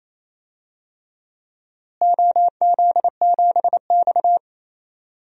Morse code, heard as a keyed tone, about 28 wpm, 700 Hz. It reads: OZ7X
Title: DXCC-100 Call Signs - 4 Characters - Repeated 3x - 28wpm